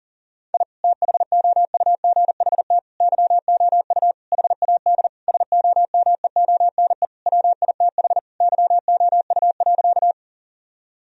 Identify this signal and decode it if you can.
Morse code, heard as a keyed tone, about 40 wpm, 700 Hz. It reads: I THOUGHT YOU HAD SOMEONE WITH YOU.
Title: Sentences from Top 1000 Words - Review Part 4 - 40wpm